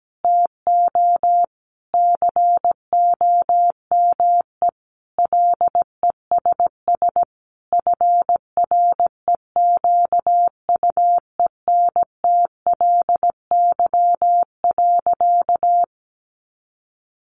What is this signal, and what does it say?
Morse code, heard as a keyed tone, about 17 wpm, 700 Hz. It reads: TO COME LESS FREQUENTLY.
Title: Sentences from Top 1300 Words 17wpm